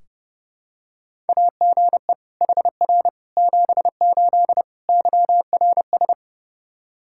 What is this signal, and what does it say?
Morse code, heard as a keyed tone, about 32 wpm, 700 Hz. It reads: AGE HR 78 YRS
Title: Practice QSO - Exchange 3 - More Info - 30wpm